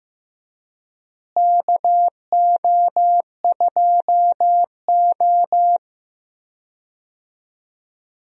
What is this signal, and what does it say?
Morse code, heard as a keyed tone, about 15 wpm, 700 Hz. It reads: KO2O
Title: USA Call Signs - 4 Characters - Repeated 3x - 15wpm